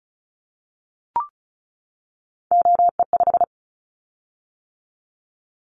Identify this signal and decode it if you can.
Morse code, heard as a keyed tone, about 36 wpm, 690 Hz. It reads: OE5
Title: Callsign Prefixes for Top 50 DXCC - 35wpm